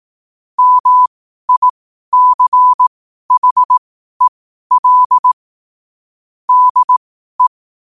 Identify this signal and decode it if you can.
Morse code, heard as a keyed tone, about 18 wpm, spaced as for 12 wpm, 1000 Hz. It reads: MICHEL DE
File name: qotdfiles / 12wpm